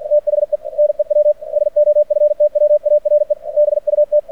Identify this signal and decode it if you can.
Morse code, heard as a keyed tone, about 50 wpm, 600 Hz. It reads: THE NEW SOFTWARE BUT
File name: archived / CRSnet